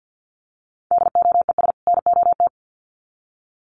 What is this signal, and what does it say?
Morse code, heard as a keyed tone, about 50 wpm, 700 Hz. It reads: DOES NOT
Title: Sets of 2 Words 50wpm